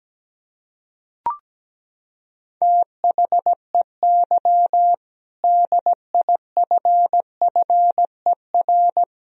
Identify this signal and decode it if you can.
Morse code, heard as a keyed tone, about 17 wpm, 700 Hz. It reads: THEY DIFFER
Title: Sentences from Top 1000 Words - Review Part 2 - 17wpm